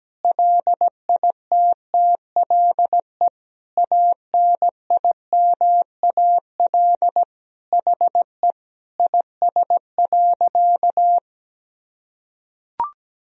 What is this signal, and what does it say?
Morse code, heard as a keyed tone, about 17 wpm, 700 Hz. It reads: LITTLE ANIMAL HE IS.
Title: Sentences from Top 1000 Words - Review Part 1 - 17wpm